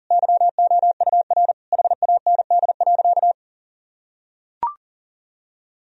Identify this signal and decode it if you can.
Morse code, heard as a keyed tone, about 40 wpm, 700 Hz. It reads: YOUR HAND.
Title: Sentences from Top 1000 Words - Review Part 2 - 40wpm